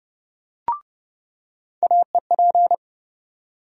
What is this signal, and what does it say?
Morse code, heard as a keyed tone, about 32 wpm, 700 Hz. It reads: AEP
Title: USA Call Sign Postfixes - 3 Letters 30wpm